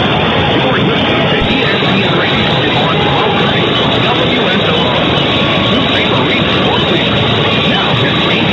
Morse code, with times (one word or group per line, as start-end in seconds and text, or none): none